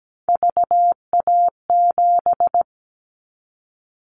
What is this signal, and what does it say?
Morse code, heard as a keyed tone, about 17 wpm, 700 Hz. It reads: VA7
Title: Callsign Prefixes for Top 50 DXCC - 17wpm